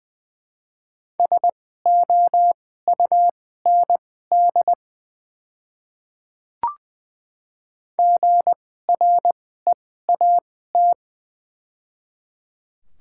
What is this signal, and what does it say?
Morse code, heard as a keyed tone, about 20 wpm, spaced as for 14 wpm, 700 Hz. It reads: SOUND GREAT
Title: 2x Character Spacing - Top 200 Words - 20wpm